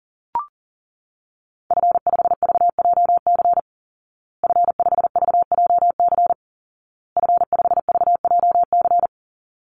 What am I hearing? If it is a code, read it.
Morse code, heard as a keyed tone, about 40 wpm, 700 Hz. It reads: F5VJC F5VJC F5VJC
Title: Call Signs Repeated 2x - 40wpm